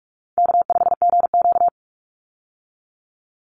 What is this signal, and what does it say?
Morse code, heard as a keyed tone, about 44 wpm, 700 Hz. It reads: K5GQ
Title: Call Signs Repeated 3x - 45wpm